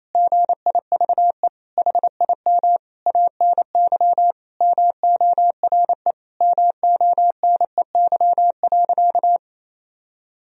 Morse code, heard as a keyed tone, about 28 wpm, 700 Hz. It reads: GIVE HIM ANY MORE MONEY.
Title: Sentences from Top 1000 Words - Review Part 4 - 28wpm